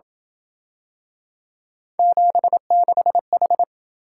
Morse code, heard as a keyed tone, about 27 wpm, 700 Hz. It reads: 76H